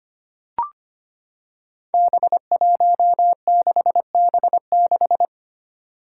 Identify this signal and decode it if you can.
Morse code, heard as a keyed tone, about 25 wpm, 700 Hz. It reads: B16B6
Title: Five Challenging Letter-Number Combinations 25wpm